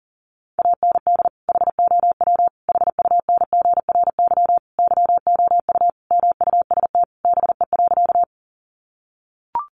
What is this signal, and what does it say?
Morse code, heard as a keyed tone, about 40 wpm, 700 Hz. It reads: AND HOW HUNGRY YOU MUST BE.
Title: Sentences from Top 1400 Words 40wpm